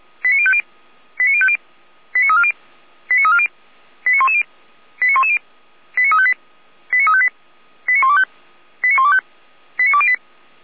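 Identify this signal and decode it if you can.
Five-tone selective call (ZVEI), heard as quick runs of five tones, tones of about 70 ms. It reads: 89050 89050 89340 89340 89110 89110 89358 89358 89125 89125 89208
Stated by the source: ZVEI five-tone selective call 89173 received by SDR, logged on the Feuerwehr Orlamünde server.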